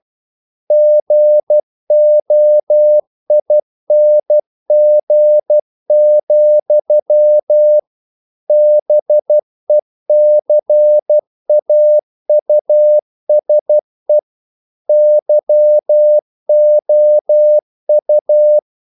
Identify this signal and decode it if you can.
Morse code, heard as a keyed tone, about 12 wpm, 600 Hz. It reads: GOING, BECAUSE YOU
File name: CW / Practice / EnglishQuotations0312